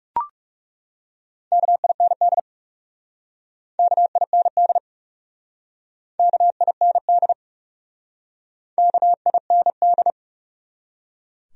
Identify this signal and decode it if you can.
Morse code, heard as a keyed tone, about 39 wpm, 700 Hz. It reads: KIND KIND KIND KIND